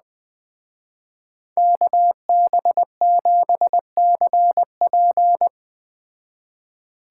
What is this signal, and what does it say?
Morse code, heard as a keyed tone, about 21 wpm, 700 Hz. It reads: KB7CP